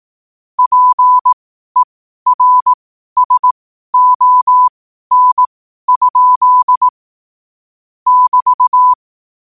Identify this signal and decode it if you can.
Morse code, heard as a keyed tone, about 18 wpm, spaced as for 12 wpm, 1000 Hz. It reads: PERSON? <BT>
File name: qotdfiles / 12wpm